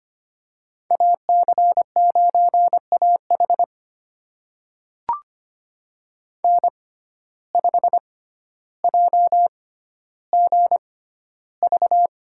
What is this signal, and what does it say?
Morse code, heard as a keyed tone, about 25 wpm, spaced as for 13 wpm, 700 Hz. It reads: AC9AH N5JGV
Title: USA Call Signs - 5 Characters - 6x Character Spacing - 25wpm